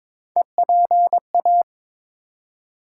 Morse code, heard as a keyed tone, about 22 wpm, 700 Hz. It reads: EPA
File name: ARRL Sections v2 - 22wpm